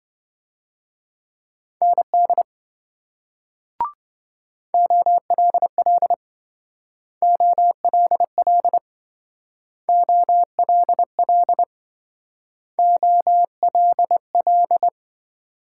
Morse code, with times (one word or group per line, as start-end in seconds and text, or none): 1.8-2.4 ND
4.7-6.1 OLL
7.2-8.8 OLL
9.9-11.6 OLL
12.8-14.9 OLL